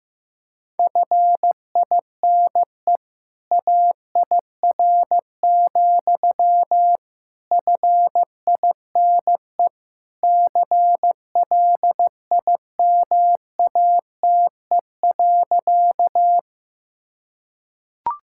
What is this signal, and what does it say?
Morse code, heard as a keyed tone, about 16 wpm, 700 Hz. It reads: FINE AIR, FINE CLIMATE.
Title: Sentences from Top 1600 Words 15wpm